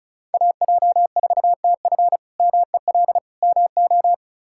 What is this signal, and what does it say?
Morse code, heard as a keyed tone, about 36 wpm, 700 Hz. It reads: AJ4TF MEL MO
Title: K1USN SST QSO Practice - 35wpm